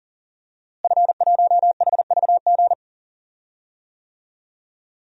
Morse code, heard as a keyed tone, about 40 wpm, 700 Hz. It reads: F1HVG
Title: DXCC-100 Call Signs - 5 Characters - Repeated 2x - 40wpm